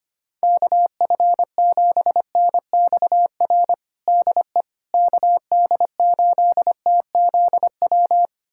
Morse code, heard as a keyed tone, about 25 wpm, 700 Hz. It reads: KF7NXR DE KD8TZW
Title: USA Callsign DE Callsign - 6 Characters - 25wpm